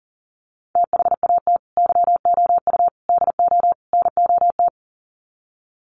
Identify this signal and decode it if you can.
Morse code, heard as a keyed tone, about 40 wpm, 700 Hz. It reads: THAT YOU DO NOT
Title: Sets of 4 Words 40wpm